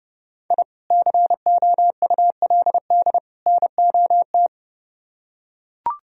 Morse code, heard as a keyed tone, about 29 wpm, 700 Hz. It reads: I COULD NOT